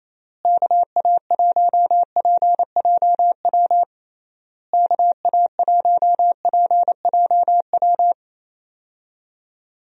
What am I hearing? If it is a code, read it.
Morse code, heard as a keyed tone, about 28 wpm, 700 Hz. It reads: KA1PJW KA1PJW
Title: USA Call Signs - 6 Characters - Repeated 3x - 28wpm